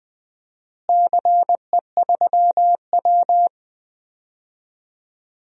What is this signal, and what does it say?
Morse code, heard as a keyed tone, about 20 wpm, 700 Hz. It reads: CE3W
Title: DXCC-100 Call Signs - 4 Characters - Repeated 3x - 20wpm